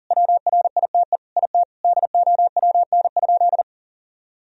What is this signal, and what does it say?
Morse code, heard as a keyed tone, about 40 wpm, 700 Hz. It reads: WRITE IT DOWN?
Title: Sentences from Top 1000 Words - Review Part 1 - 40wpm